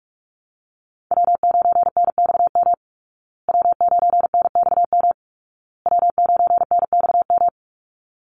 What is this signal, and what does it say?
Morse code, heard as a keyed tone, about 44 wpm, 700 Hz. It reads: W9NXM W9NXM W9NXM
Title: Call Signs Repeated 2x - 45wpm